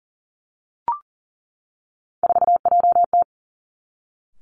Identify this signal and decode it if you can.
Morse code, heard as a keyed tone, about 37 wpm, 700 Hz. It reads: 4JT
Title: Three Challenging Letter-Number Combinations 40wpm